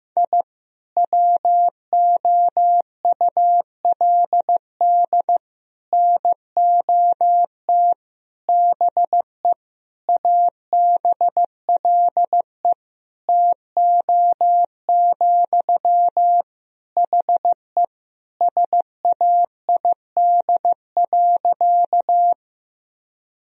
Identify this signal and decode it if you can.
Morse code, heard as a keyed tone, about 16 wpm, 700 Hz. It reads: I WOULD NOT BE ABLE TO, HE SAID.
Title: Sentences from Top 1000 Words - Review Part 2 - 15wpm